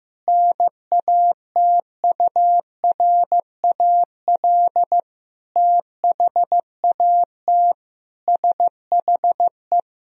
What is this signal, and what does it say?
Morse code, heard as a keyed tone, about 15 wpm, 700 Hz. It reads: NATURAL THAT SHE
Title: Sentences from Top 1000 Words - Review Part 4 - 15wpm